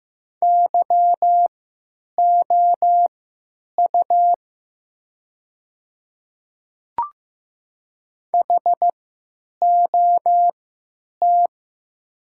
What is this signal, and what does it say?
Morse code, heard as a keyed tone, about 15 wpm, spaced as for 9 wpm, 700 Hz. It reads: YOU HOT
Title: 3x Character Spacing - Top 100 Words - 15wpm